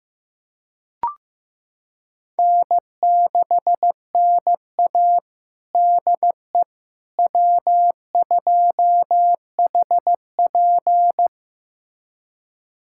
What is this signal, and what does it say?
Morse code, heard as a keyed tone, about 16 wpm, 700 Hz. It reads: N6NA DE W2HP
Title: USA Callsign DE Callsign - 4 Characters - 15wpm